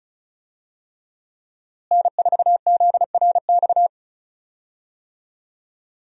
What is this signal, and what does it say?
Morse code, heard as a keyed tone, about 36 wpm, 700 Hz. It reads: N4ZRX
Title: USA Call Signs - 5 Characters - Repeated 3x - 35wpm